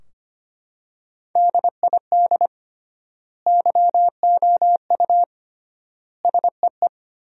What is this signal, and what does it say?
Morse code, heard as a keyed tone, about 25 wpm, 700 Hz. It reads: DID YOU SEE
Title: 3x Word-Spacing - Sentences from Top 100 Words - 25wpm